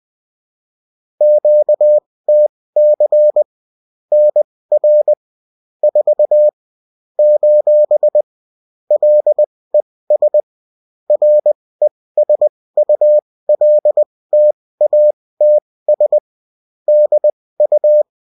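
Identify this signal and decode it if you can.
Morse code, heard as a keyed tone, about 20 wpm, spaced as for 16 wpm, 600 Hz. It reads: QTC NR 4 : LES RESULTATS DU
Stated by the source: Morse code bulletin of a French amateur CW union, sent as text and in CW.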